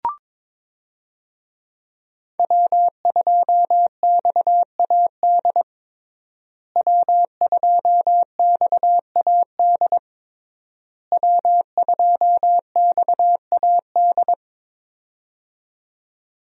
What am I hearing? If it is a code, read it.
Morse code, heard as a keyed tone, about 22 wpm, 690 Hz. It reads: W2XAD W2XAD W2XAD
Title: USA Call Signs - 5 Characters - Repeated 3x - 22wpm